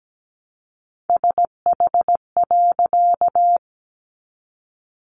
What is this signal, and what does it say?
Morse code, heard as a keyed tone, about 17 wpm, 700 Hz. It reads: SH.
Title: A Sound Of Thunder 17wpm